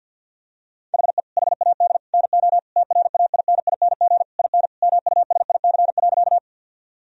Morse code, heard as a keyed tone, about 50 wpm, 700 Hz. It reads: HE HAD NO TRAINING IN MUSIC.
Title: Sentences from Top 1000 Words - Review Part 4 - 50wpm